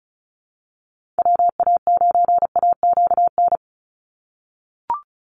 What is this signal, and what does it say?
Morse code, heard as a keyed tone, about 36 wpm, 700 Hz. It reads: WA9AQN
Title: Call Signs - 35wpm